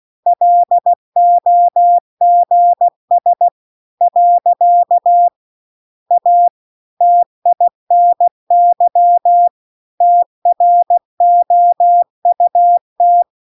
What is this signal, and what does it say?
Morse code, heard as a keyed tone, about 16 wpm, 700 Hz. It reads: LOGS . A TINY TROUT
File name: CW-The-Big-Twoheaded-River-2-16WPM